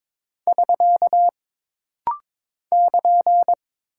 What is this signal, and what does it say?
Morse code, heard as a keyed tone, about 22 wpm, 700 Hz. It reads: <SK> <KN>
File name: practice / 22wpm / Single Punctuation-Prosign - Rapid-Fire - 22wpm